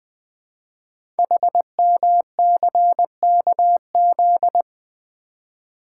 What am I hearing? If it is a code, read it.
Morse code, heard as a keyed tone, about 20 wpm, 700 Hz. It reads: HMCKZ
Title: Code Groups - 5 Letters - 20wpm